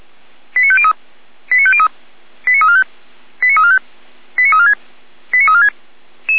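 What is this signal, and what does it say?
Five-tone selective call (ZVEI), heard as quick runs of five tones, tones of about 70 ms. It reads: 89503 89503 89345 89345 89346 89346
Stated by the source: ZVEI five-tone call received by SDR, feuerwehr-orlamuende.org fire brigade.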